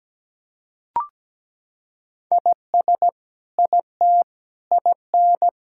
Morse code, heard as a keyed tone, about 17 wpm, 700 Hz. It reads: IS IT IN
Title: Sentences from Top 1600 Words 17wpm